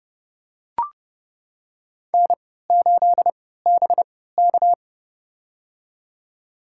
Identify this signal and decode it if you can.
Morse code, heard as a keyed tone, about 30 wpm, spaced as for 17 wpm, 700 Hz. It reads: N8BK
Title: USA Call Signs - 4 Characters - 3x Character Spacing - 30wpm